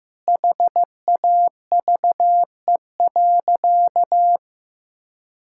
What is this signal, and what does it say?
Morse code, heard as a keyed tone, about 16 wpm, 700 Hz. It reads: HAVE.
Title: Sentences from Top 1000 Words - Review Part 6 - 15wpm